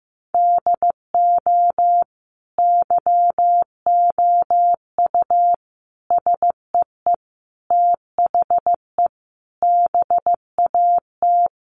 Morse code, heard as a keyed tone, about 15 wpm, 700 Hz. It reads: DO YOU SEE THE BAT
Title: Sentences from Top 1500 Words 15wpm